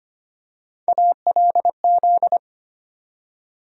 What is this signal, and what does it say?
Morse code, heard as a keyed tone, about 25 wpm, 700 Hz. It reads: ALZ
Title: USA Call Sign Postfixes - 3 Letters 25wpm